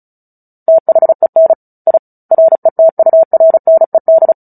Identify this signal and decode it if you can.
Morse code, heard as a keyed tone, about 34 wpm, 650 Hz. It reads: THEN I RETURNED
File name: Book Two / 09-Chapter 9 - 35 WPM